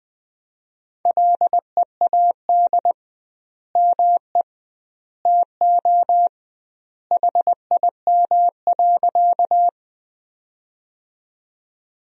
Morse code, heard as a keyed tone, about 20 wpm, 700 Hz. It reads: LEAD ME TO HIM.
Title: 2x Word-Spacing - Sentences from Top 500 Words - 20wpm